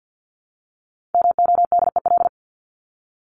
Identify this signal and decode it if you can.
Morse code, heard as a keyed tone, about 48 wpm, 700 Hz. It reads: MODEL
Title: Top 501-750 Words 50wpm